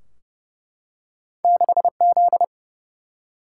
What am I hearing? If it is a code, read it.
Morse code, heard as a keyed tone, about 29 wpm, 700 Hz. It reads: 6Z
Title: Single-Letter Single-Number 30wpm